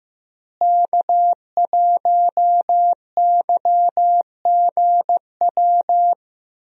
Morse code, heard as a keyed tone, about 15 wpm, 700 Hz. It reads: K1YGW
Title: USA Call Signs - 5 Characters - Repeated 3x - 15wpm